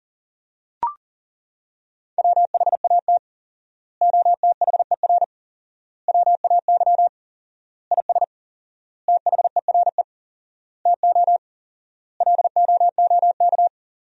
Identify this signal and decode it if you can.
Morse code, heard as a keyed tone, about 40 wpm, 700 Hz. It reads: WHAT OTHER WAY IS THERE TO LOOK
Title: 4x Word-Spacing - Sentences from Top 100 Words - 40wpm